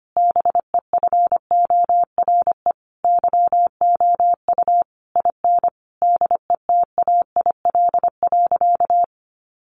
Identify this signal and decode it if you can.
Morse code, heard as a keyed tone, about 25 wpm, 700 Hz. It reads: BEFORE YOU IN DETAIL.
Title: Sentences from Top 1500 Words 25wpm